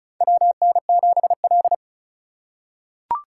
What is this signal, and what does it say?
Morse code, heard as a keyed tone, about 36 wpm, 700 Hz. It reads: WN7L
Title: USA Call Signs - 4 Characters - 4x Character Spacing - 35wpm